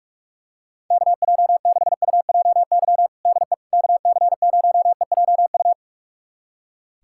Morse code, heard as a keyed tone, about 41 wpm, 700 Hz. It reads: KJ6UJY DE KC0EJU
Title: USA Callsign DE Callsign - 6 Characters - 45wpm